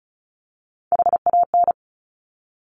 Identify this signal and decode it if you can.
Morse code, heard as a keyed tone, about 36 wpm, 700 Hz. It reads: HAN
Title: Top Three Letter Combinations in Words 35wpm